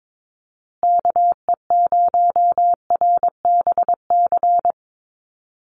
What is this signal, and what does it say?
Morse code, heard as a keyed tone, about 22 wpm, 700 Hz. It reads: KE0RBC
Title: USA Call Signs Repeated 2x 22wpm